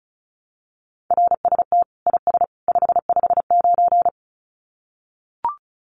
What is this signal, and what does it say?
Morse code, heard as a keyed tone, about 34 wpm, 700 Hz. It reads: RST IS 559
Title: Common QSO Phrases 35wpm